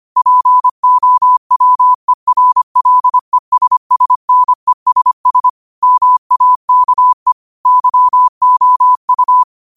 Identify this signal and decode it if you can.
Morse code, heard as a keyed tone, about 25 wpm, 1000 Hz. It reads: POWERLESSNESS MAKE YOU
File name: qotdfiles / 25wpm